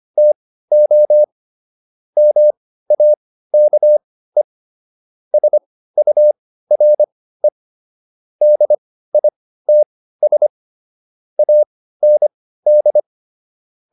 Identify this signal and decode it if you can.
Morse code, heard as a keyed tone, about 25 wpm, spaced as for 15 wpm, 600 Hz. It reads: TO MAKE SURE DITS AND